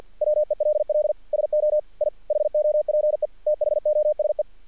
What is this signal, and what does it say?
Morse code, heard as a keyed tone, about 47 wpm, 600 Hz. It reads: WELL SO I HOPE THOSE
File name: archived / CRSnet